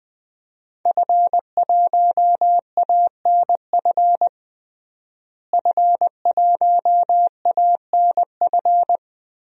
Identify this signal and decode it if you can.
Morse code, heard as a keyed tone, about 20 wpm, 700 Hz. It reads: F1ANF F1ANF
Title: DXCC-100 Call Signs - 5 Characters - Repeated 3x - 20wpm